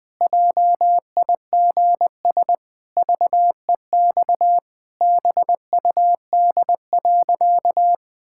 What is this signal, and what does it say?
Morse code, heard as a keyed tone, about 20 wpm, 700 Hz. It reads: JIGS VEX BUD.
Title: Pangram Challenge v2 - 20wpm